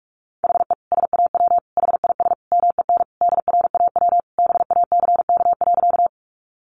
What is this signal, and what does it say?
Morse code, heard as a keyed tone, about 44 wpm, 700 Hz. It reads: HE SAW HIS MEN DRAW BACK.
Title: Sentences from Top 300 Words 45wpm